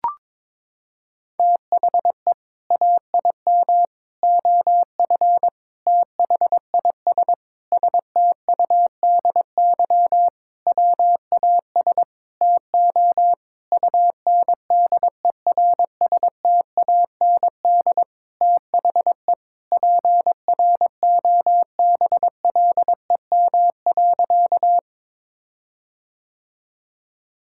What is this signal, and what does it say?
Morse code, heard as a keyed tone, about 22 wpm, 700 Hz. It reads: THE AIM OF THIS STUDY WAS TO UNDERSTAND THE PROBLEM.
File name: Sentences with 7-Word Phrases - 22wpm